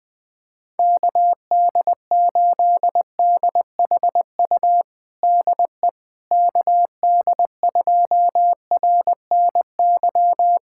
Morse code, heard as a keyed tone, about 20 wpm, 700 Hz. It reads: KD8DHU DE KD2RNY
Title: USA Callsign DE Callsign 20wpm